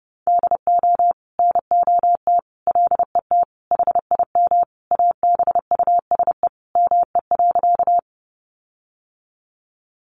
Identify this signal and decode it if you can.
Morse code, heard as a keyed tone, about 30 wpm, 700 Hz. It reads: DO NOT LET HIM ABUSE ME.
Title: Sentences from Top 1300 Words 30wpm